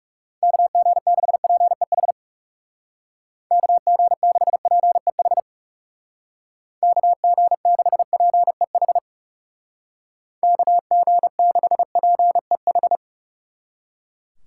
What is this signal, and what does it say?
Morse code, heard as a keyed tone, about 37 wpm, 700 Hz. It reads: KG6PEH KG6PEH KG6PEH KG6PEH
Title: Speed Racing - USA Call Signs 30wpm